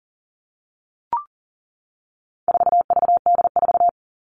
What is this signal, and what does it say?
Morse code, heard as a keyed tone, about 40 wpm, 700 Hz. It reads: 4VD4
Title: All Cut Numbers - 4-Digit 40wpm